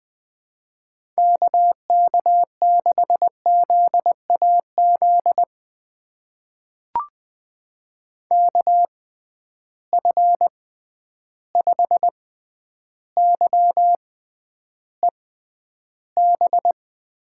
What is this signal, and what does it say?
Morse code, heard as a keyed tone, about 20 wpm, spaced as for 10 wpm, 700 Hz. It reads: KK6ZAZ KF5YEB
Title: USA Call Signs - 6 Characters - 6x Character Spacing - 20wpm